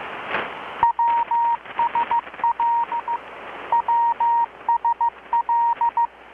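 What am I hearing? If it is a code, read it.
Morse code, heard as a keyed tone, about 15 wpm, 940 Hz. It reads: WSL WSL
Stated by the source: Morse from KPH's Night of Nights XX maritime radiotelegraphy event, received in Chiapas, Mexico about 3,800 km away on a Kenwood R-600 with a grounded 10 m long-wire antenna.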